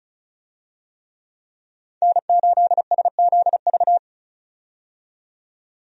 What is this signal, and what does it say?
Morse code, heard as a keyed tone, about 36 wpm, 700 Hz. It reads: N8SZV
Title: USA Call Signs - 5 Characters - 35wpm